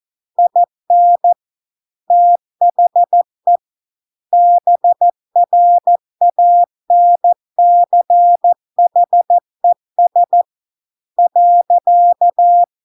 Morse code, heard as a keyed tone, about 14 wpm, 700 Hz. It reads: IN THE BRANCHES .